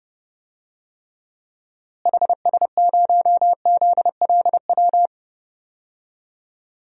Morse code, heard as a keyed tone, about 30 wpm, 700 Hz. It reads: HS0ZLW
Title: DXCC-100 Call Signs - 6 Characters - Repeated 2x - 30wpm